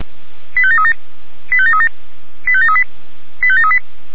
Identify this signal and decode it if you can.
Five-tone selective call (ZVEI), heard as quick runs of five tones, tones of about 80 ms. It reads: 85737 85737 85738 85738
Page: /464/SDR/ZVEI/